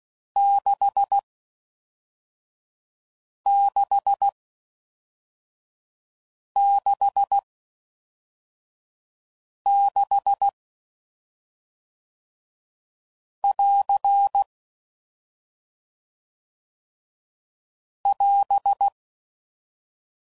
Morse code, heard as a keyed tone, about 16 wpm, spaced as for 3 wpm, 800 Hz. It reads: 6666<AR> <AS>